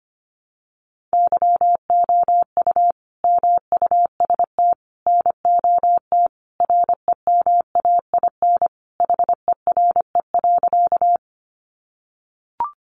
Morse code, heard as a keyed tone, about 25 wpm, 700 Hz. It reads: YOU MUST NOT REMAIN HERE.
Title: Sentences from Top 1200 Words 25wpm